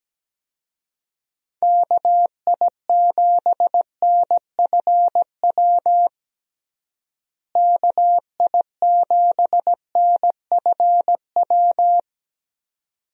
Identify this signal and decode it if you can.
Morse code, heard as a keyed tone, about 17 wpm, 700 Hz. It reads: KI7NFW KI7NFW
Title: USA Call Signs - 6 Characters - Repeated 2x - 17wpm